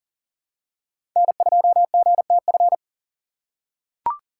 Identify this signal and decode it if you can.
Morse code, heard as a keyed tone, about 37 wpm, 700 Hz. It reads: N2GTF
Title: USA Call Signs - 5 Characters - 6x Character Spacing - 40wpm